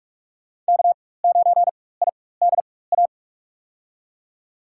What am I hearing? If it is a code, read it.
Morse code, heard as a keyed tone, about 43 wpm, spaced as for 21 wpm, 700 Hz. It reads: K9IDA
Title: USA Call Signs - 4x Character Spacing - 45wpm